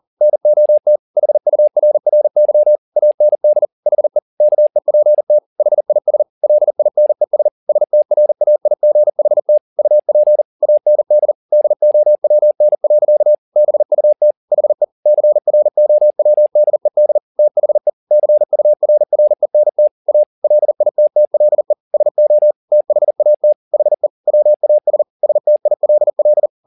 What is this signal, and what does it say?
Morse code, heard as a keyed tone, about 40 wpm, 600 Hz. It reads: NOT HURRY AND HE KEPT HIS LINES STRAIGHT UP AND DOWN. BUT HE CROWDED THE CURRENT A LITTLE SO THAT HE WAS STILL